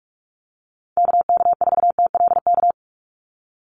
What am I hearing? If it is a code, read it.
Morse code, heard as a keyed tone, about 44 wpm, 700 Hz. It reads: KK4TLK
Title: USA Call Signs 45wpm